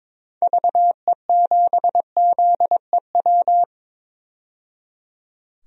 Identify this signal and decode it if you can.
Morse code, heard as a keyed tone, about 22 wpm, 700 Hz. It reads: VE7ZEW
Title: DXCC-100 Call Signs - 6 Characters - 22wpm